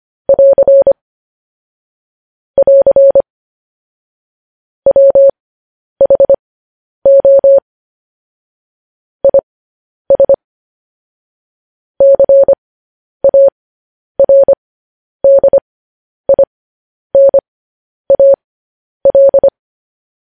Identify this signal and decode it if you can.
Morse code, heard as a keyed tone, about 25 wpm, spaced as for 10 wpm, 550 Hz. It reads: <AR> <AR> WHO IS CARDINAL
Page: News Headlines in Morse Code at 10 WPM